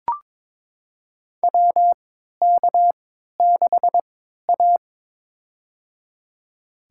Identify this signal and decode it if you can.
Morse code, heard as a keyed tone, about 22 wpm, spaced as for 12 wpm, 700 Hz. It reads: WK6A
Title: USA Call Signs - 4 Characters - 3x Character Spacing - 22wpm